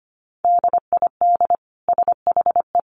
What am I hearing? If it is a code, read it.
Morse code, heard as a keyed tone, about 25 wpm, 700 Hz. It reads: DID SHE
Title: Sentences from Top 1200 Words 25wpm